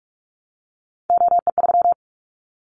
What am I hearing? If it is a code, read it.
Morse code, heard as a keyed tone, about 44 wpm, 700 Hz. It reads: OE3
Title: Callsign Prefixes for Top 100 DXCC - 45wpm